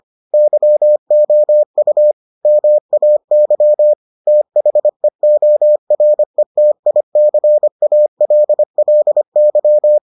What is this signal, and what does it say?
Morse code, heard as a keyed tone, about 25 wpm, 600 Hz. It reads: YOU MAY THEORETICALLY